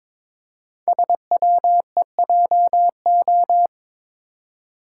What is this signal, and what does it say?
Morse code, heard as a keyed tone, about 22 wpm, 700 Hz. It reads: SWEJO
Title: Code Groups - 5 Letters - 22wpm